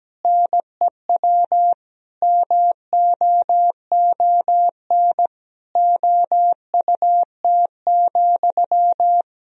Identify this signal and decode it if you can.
Morse code, heard as a keyed tone, about 17 wpm, 700 Hz. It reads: NEW MOON OUT,
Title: Sentences from Top 500 Words and Abbreviations 17wpm